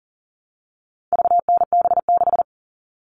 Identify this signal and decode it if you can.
Morse code, heard as a keyed tone, about 40 wpm, 700 Hz. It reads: VNB6
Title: All Cut Numbers - 4-Digit 40wpm